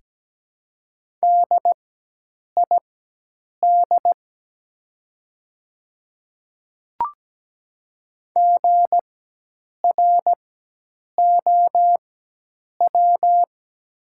Morse code, heard as a keyed tone, about 17 wpm, spaced as for 8 wpm, 700 Hz. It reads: DID GROW
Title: 4x Character Spacing - Top 300 Words - 17wpm